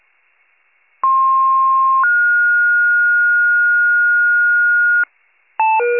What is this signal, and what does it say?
Two-tone page, heard as a one-second tone, then a three-second tone, about 1060 then 1430 Hz, with noise. It is heard from 1.0 to 5.0 s.